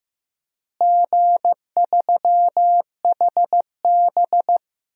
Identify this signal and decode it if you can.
Morse code, heard as a keyed tone, about 15 wpm, 700 Hz. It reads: G3HB
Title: DXCC-100 Call Signs - 4 Characters - Repeated 3x - 15wpm